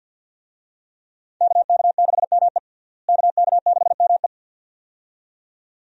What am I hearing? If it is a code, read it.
Morse code, heard as a keyed tone, about 52 wpm, 700 Hz. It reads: KK6ME KK6ME
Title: USA Call Signs - 5 Characters - Repeated 2x - 50wpm